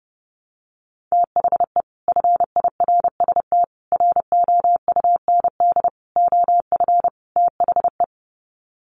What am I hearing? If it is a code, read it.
Morse code, heard as a keyed tone, about 30 wpm, 700 Hz. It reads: THE FIRST ROUND OF THE
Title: Sets of 5 Words Encore 30wpm